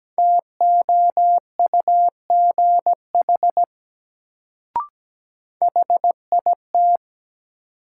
Morse code, heard as a keyed tone, about 17 wpm, 700 Hz. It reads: TOUGH HIT
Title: Sentences from Top 1200 Words - New Word Drill Alt - 17wpm